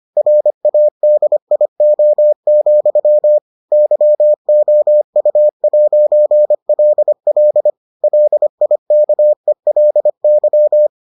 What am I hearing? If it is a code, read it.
Morse code, heard as a keyed tone, about 25 wpm, 600 Hz. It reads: RADIO, YOU'LL LIKELY